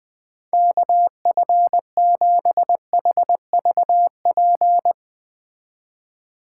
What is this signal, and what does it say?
Morse code, heard as a keyed tone, about 20 wpm, 700 Hz. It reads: KF7HVP